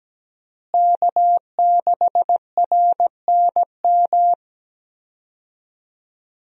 Morse code, heard as a keyed tone, about 17 wpm, 700 Hz. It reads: K6RNM
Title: USA Call Signs - 5 Characters - Repeated 3x - 17wpm